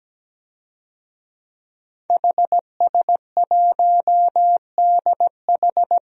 Morse code, heard as a keyed tone, about 17 wpm, 700 Hz. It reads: HS1DH
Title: DXCC-100 Call Signs - 5 Characters - Repeated 2x - 17wpm